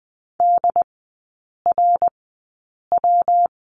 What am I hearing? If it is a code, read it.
Morse code, heard as a keyed tone, about 20 wpm, spaced as for 8 wpm, 700 Hz. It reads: DRW
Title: 114 Lesson 19e - Letter w - Three Characters 20wpm